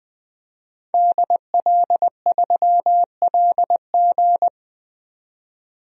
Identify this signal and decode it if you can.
Morse code, heard as a keyed tone, about 20 wpm, 700 Hz. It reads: DL3LG
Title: DXCC-100 Call Signs - 5 Characters - 20wpm